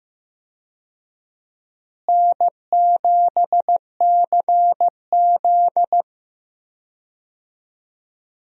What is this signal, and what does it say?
Morse code, heard as a keyed tone, about 15 wpm, 700 Hz. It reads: N7CZ
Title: USA Call Signs - 4 Characters - Repeated 3x - 15wpm